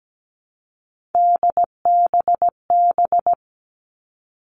Morse code, heard as a keyed tone, about 17 wpm, 700 Hz. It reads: DBB
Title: All Cut Numbers - 3-Digit 17wpm